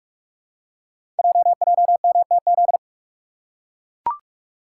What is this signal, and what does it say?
Morse code, heard as a keyed tone, about 44 wpm, 700 Hz. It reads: JJMTZ
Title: Code Groups - 5 Letters - 45wpm